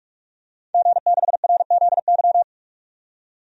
Morse code, heard as a keyed tone, about 46 wpm, 700 Hz. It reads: G6RZY
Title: DXCC-100 Call Signs - 5 Characters - 45wpm